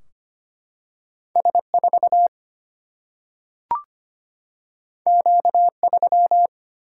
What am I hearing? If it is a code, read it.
Morse code, heard as a keyed tone, about 25 wpm, 700 Hz. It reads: S4 Q3
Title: Single-Letter Single-Number 25wpm